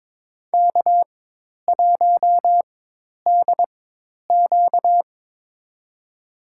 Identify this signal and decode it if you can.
Morse code, heard as a keyed tone, about 21 wpm, spaced as for 10 wpm, 700 Hz. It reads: K1DQ
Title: USA Call Signs - 4 Characters - 4x Character Spacing - 22wpm